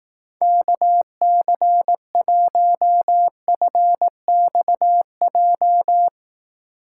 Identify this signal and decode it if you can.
Morse code, heard as a keyed tone, about 18 wpm, 700 Hz. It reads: KC1FXJ
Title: Speed Racing - USA 2x3 Call Signs 15wpm